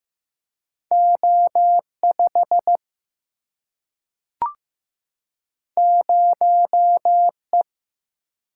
Morse code, heard as a keyed tone, about 15 wpm, 700 Hz. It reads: O5 0E
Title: Single-Letter Single-Number 15wpm